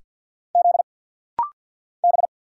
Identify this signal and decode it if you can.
Morse code, heard as a keyed tone, about 50 wpm, 690 Hz. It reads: Z B
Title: Single Letters - Rapid-Fire 50wpm